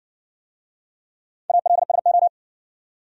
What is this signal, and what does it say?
Morse code, heard as a keyed tone, about 59 wpm, 700 Hz. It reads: ALSO